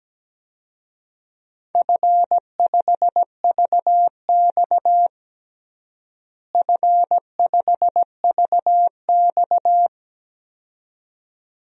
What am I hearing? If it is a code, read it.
Morse code, heard as a keyed tone, about 17 wpm, 700 Hz. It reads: F5VX F5VX
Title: DXCC-100 Call Signs - 4 Characters - Repeated 2x - 17wpm